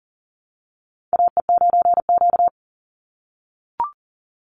Morse code, heard as a keyed tone, about 40 wpm, 700 Hz. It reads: AE9Q
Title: NA SOTA Chaser Callsigns 40wpm